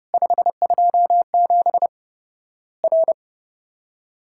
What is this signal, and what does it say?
Morse code, heard as a keyed tone, about 30 wpm, 700 Hz. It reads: 527 R
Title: Worked All Europe DX Contest - QTC Exchange - 30wpm